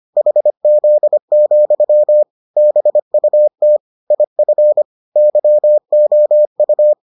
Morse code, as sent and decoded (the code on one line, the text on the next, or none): .... --.. --..-- / -... ..- - / .. ..-. / -.-- --- ..-
HZ, BUT IF YOU